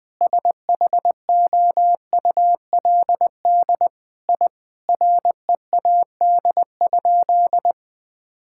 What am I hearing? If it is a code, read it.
Morse code, heard as a keyed tone, about 20 wpm, 700 Hz. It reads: SHOULD I READ?
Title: Sentences from Top 1000 Words - Review Part 5 - 20wpm